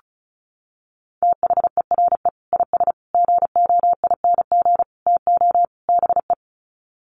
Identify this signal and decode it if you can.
Morse code, heard as a keyed tone, about 34 wpm, 700 Hz. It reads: THERE IS GOING TO BE
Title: Sets of 5 Words Encore 35wpm